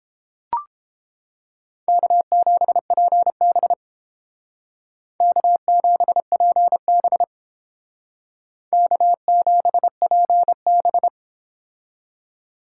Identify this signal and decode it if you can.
Morse code, heard as a keyed tone, about 30 wpm, 700 Hz. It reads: K7PB K7PB K7PB